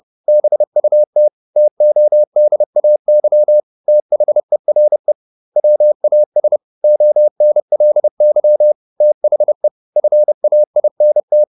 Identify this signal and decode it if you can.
Morse code, heard as a keyed tone, about 30 wpm, 600 Hz. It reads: BUT TODAY THERE WAS ONLY THE FAINT
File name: the old man and the sea / the old man and the sea 1 30wpm